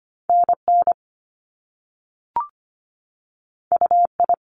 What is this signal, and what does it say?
Morse code, heard as a keyed tone, about 25 wpm, 700 Hz. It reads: NN UI
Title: Top Two Letter Combinations in Words 25wpm